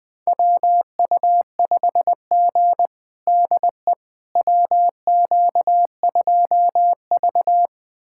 Morse code, heard as a keyed tone, about 21 wpm, 700 Hz. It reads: WU5G DE WQ2V
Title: USA Callsign DE Callsign - 4 Characters - 20wpm